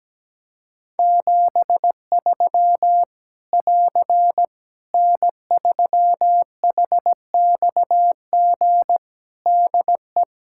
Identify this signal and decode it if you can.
Morse code, heard as a keyed tone, about 17 wpm, 700 Hz. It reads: 73 <AR> N3HXG DE
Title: K1USN SST QSO Practice 17wpm